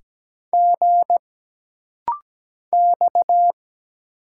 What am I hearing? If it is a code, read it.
Morse code, heard as a keyed tone, about 17 wpm, 700 Hz. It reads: G X
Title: Single Letters - Rapid-Fire 17wpm